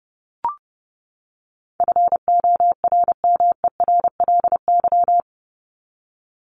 Morse code, heard as a keyed tone, about 30 wpm, 700 Hz. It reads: FORMERLY
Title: LY Suffix - Top 100 Words with Base Words 30wpm